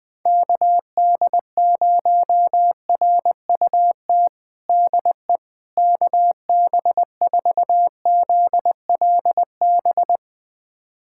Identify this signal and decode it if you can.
Morse code, heard as a keyed tone, about 20 wpm, 700 Hz. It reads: KD0RUT DE KB4ZLB
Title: USA Callsign DE Callsign - 6 Characters - 20wpm